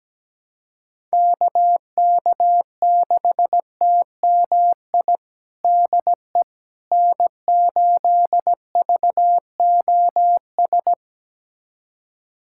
Morse code, heard as a keyed tone, about 17 wpm, 700 Hz. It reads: KK6TMI DE N8VOS
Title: USA Callsign DE Callsign 17wpm